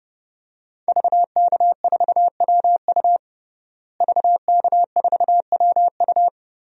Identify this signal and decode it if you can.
Morse code, heard as a keyed tone, about 30 wpm, 700 Hz. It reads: VK4WU VK4WU
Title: DXCC-100 Call Signs - 5 Characters - Repeated 3x - 30wpm